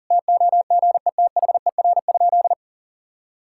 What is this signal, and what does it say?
Morse code, heard as a keyed tone, about 39 wpm, 700 Hz. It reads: TOGETHER?
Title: Sentences from Top 1600 Words 40wpm